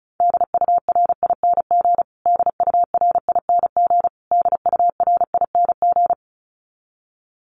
Morse code, heard as a keyed tone, about 34 wpm, 700 Hz. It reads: DURING DURING DURING
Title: Top 500 Words Repeated 35wpm